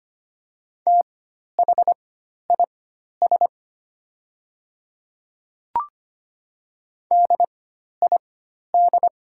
Morse code, heard as a keyed tone, about 25 wpm, spaced as for 12 wpm, 700 Hz. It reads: THIS DID
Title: 4x Character Spacing - Top 100 Words - 25wpm